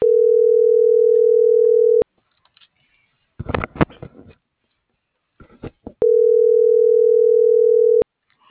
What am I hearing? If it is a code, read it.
Ringback tone (North America), 440 + 480 Hz.